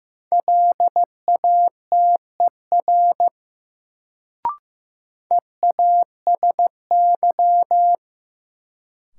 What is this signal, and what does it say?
Morse code, heard as a keyed tone, about 15 wpm, 700 Hz. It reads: LATER EASY
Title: Sentences from Top 700 Words - New Word Drill - 15wpm